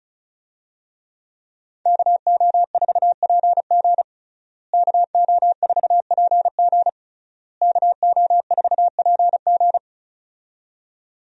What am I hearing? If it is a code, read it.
Morse code, heard as a keyed tone, about 36 wpm, 700 Hz. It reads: KO4PG KO4PG KO4PG
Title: USA Call Signs - 5 Characters - Repeated 3x - 35wpm